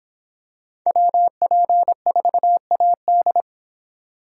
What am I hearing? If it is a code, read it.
Morse code, heard as a keyed tone, about 26 wpm, 700 Hz. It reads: WP4AD